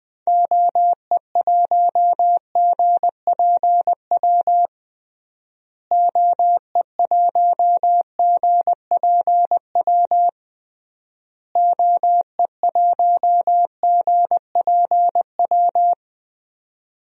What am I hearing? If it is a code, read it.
Morse code, heard as a keyed tone, about 20 wpm, 700 Hz. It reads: OE1GPW OE1GPW OE1GPW
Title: DXCC-100 Call Signs - 6 Characters - Repeated 3x - 20wpm